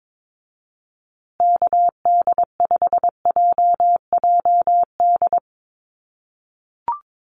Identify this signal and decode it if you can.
Morse code, heard as a keyed tone, about 22 wpm, 700 Hz. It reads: KD5JJD
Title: USA Call Signs Repeated 3x 22wpm